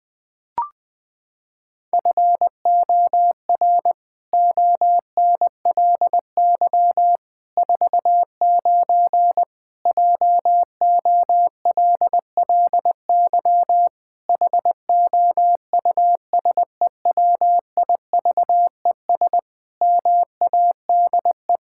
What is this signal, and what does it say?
Morse code, heard as a keyed tone, about 21 wpm, 700 Hz. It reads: FOR ONLY 49 JOLLY HOUSEWIVES MADE
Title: Pangram Challenge v2 - 20wpm